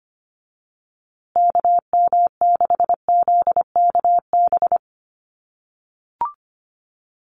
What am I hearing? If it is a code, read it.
Morse code, heard as a keyed tone, about 25 wpm, 700 Hz. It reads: KM6ZKB
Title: USA Call Signs Repeated 2x 25wpm